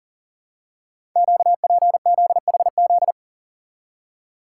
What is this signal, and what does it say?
Morse code, heard as a keyed tone, about 43 wpm, 700 Hz. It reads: QPZHZ
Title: Code Groups - 5 Letters - 40wpm